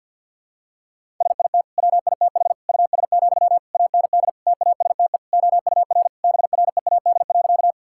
Morse code, heard as a keyed tone, about 51 wpm, 700 Hz. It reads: SIT WITH US, AND TASTE OUR BREAD.
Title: Sentences from Top 1600 Words 50wpm